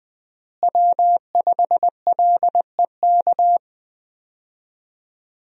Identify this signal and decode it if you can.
Morse code, heard as a keyed tone, about 20 wpm, 700 Hz. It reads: W5LEK
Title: USA Call Signs - 5 Characters - Repeated 3x - 20wpm